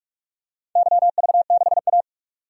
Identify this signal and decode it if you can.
Morse code, heard as a keyed tone, about 46 wpm, 700 Hz. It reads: YV6A
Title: DXCC-100 Call Signs - 4 Characters - 45wpm